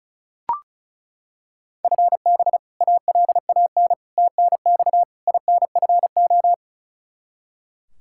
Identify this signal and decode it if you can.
Morse code, heard as a keyed tone, about 34 wpm, 700 Hz. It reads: FB ALAN TNX INFO
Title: Practice QSO - Exchange 3 - More Info - 35wpm